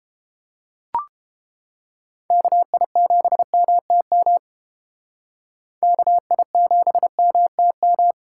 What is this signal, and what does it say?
Morse code, heard as a keyed tone, about 28 wpm, 700 Hz. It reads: KI7MTM KI7MTM